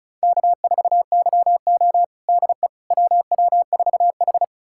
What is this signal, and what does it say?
Morse code, heard as a keyed tone, about 36 wpm, 700 Hz. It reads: K4YO DE WW4H
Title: USA Callsign DE Callsign - 4 Characters - 35wpm